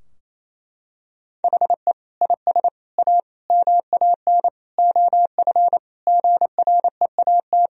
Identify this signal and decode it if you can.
Morse code, heard as a keyed tone, about 28 wpm, 700 Hz. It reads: HE IS A MAN OF GREAT
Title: Sentences from Top 1000 Words - Review Part 3 - 28wpm